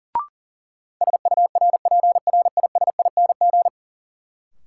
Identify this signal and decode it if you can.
Morse code, heard as a keyed tone, about 40 wpm, 700 Hz. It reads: SURPRISING
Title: Sets of 5 Words - New Word Drill - 40wpm